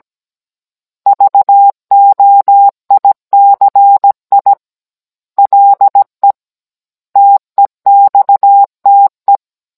Morse code, heard as a keyed tone, about 17 wpm, 800 Hz. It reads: VOICI LE TEXTE
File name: Cours-NR19-16mm